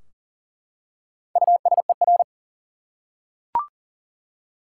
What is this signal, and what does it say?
Morse code, heard as a keyed tone, about 40 wpm, 700 Hz. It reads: USER
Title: Four Letter Words - 40wpm